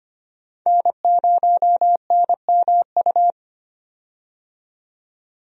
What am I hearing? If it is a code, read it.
Morse code, heard as a keyed tone, about 25 wpm, 700 Hz. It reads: N0NMU
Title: USA Call Signs - 5 Characters - 25wpm